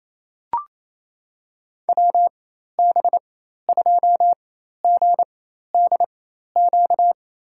Morse code, heard as a keyed tone, about 28 wpm, spaced as for 13 wpm, 700 Hz. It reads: WB2GDQ
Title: USA Call Signs - 6 Characters - 4x Character Spacing - 28wpm